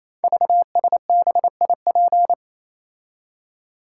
Morse code, heard as a keyed tone, about 27 wpm, 700 Hz. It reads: VSBIP
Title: Code Groups - 5 Letter:Numbers - 28wpm